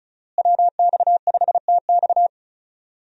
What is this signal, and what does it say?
Morse code, heard as a keyed tone, about 36 wpm, 700 Hz. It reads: WX5TX
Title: USA Call Signs - 5 Characters - Repeated 3x - 35wpm